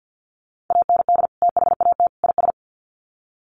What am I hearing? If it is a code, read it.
Morse code, heard as a keyed tone, about 50 wpm, 700 Hz. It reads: AND THAT IS
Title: Sets of 3 Words 50wpm